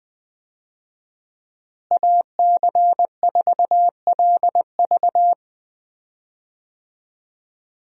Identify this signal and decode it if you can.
Morse code, heard as a keyed tone, about 20 wpm, 700 Hz. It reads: AC4LV
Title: USA Call Signs - 5 Characters - Repeated 2x - 20wpm